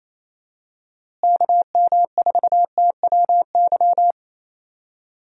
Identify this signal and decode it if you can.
Morse code, heard as a keyed tone, about 28 wpm, 700 Hz. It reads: KM4TWY
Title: USA Call Signs - 6 Characters - Repeated 2x - 28wpm